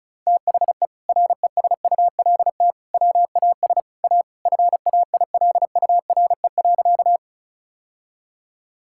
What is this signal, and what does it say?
Morse code, heard as a keyed tone, about 34 wpm, 700 Hz. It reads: THE RESULT WAS A FAILURE.
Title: Sentences from Top 1000 Words - Review Part 6 - 35wpm